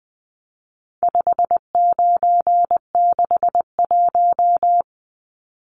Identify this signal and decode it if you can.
Morse code, heard as a keyed tone, about 20 wpm, 700 Hz. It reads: 5961
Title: Numbers - 4-Digit 20wpm